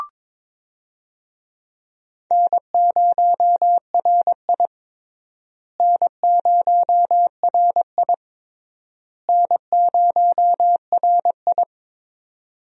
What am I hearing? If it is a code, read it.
Morse code, heard as a keyed tone, about 22 wpm, 700 Hz. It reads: N0RI N0RI N0RI
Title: USA Call Signs - 4 Characters - Repeated 3x - 22wpm